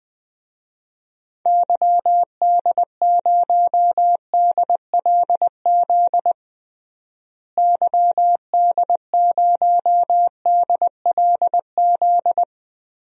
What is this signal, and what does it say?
Morse code, heard as a keyed tone, about 20 wpm, 700 Hz. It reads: YD0DLZ YD0DLZ
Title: DXCC-100 Call Signs - 6 Characters - Repeated 3x - 20wpm